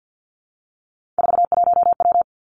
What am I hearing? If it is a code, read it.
Morse code, heard as a keyed tone, about 50 wpm, 700 Hz. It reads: 41W
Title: Three Challenging Letter-Number Combinations 50wpm